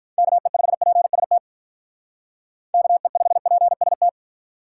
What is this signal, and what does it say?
Morse code, heard as a keyed tone, about 43 wpm, 700 Hz. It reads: KE5PST KE5PST
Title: Speed Racing - USA Call Signs 35wpm